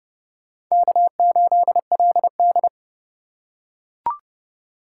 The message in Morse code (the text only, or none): K8LD